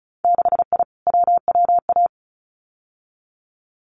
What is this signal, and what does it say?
Morse code, heard as a keyed tone, about 36 wpm, 700 Hz. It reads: 6I WWA
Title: Winter Field Day Exchanges Encore 35wpm